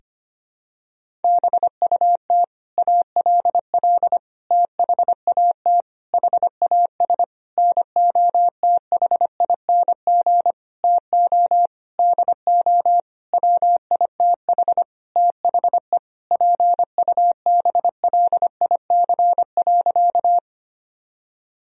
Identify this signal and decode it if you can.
Morse code, heard as a keyed tone, about 25 wpm, 700 Hz. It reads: BUT ALL THAT HAS NOTHING TO DO WITH THE PUBLIC.
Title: Sentences from Top 1000 Words - Review Part 3 - 25wpm